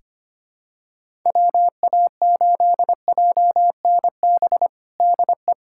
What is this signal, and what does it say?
Morse code, heard as a keyed tone, about 25 wpm, 700 Hz. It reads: WA8JNB DE
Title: USA Callsign DE Callsign - 6 Characters - 25wpm